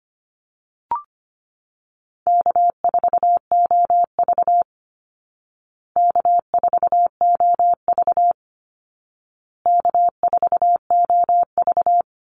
Morse code, heard as a keyed tone, about 25 wpm, 700 Hz. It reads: K4OV K4OV K4OV
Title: Easy Call Signs Repeated 3x - 25wpm